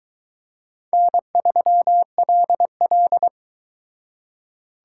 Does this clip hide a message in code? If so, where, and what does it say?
Yes: Morse code from 0.9 to 3.3 s, heard as a keyed tone, about 23 wpm, 700 Hz. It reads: N3LL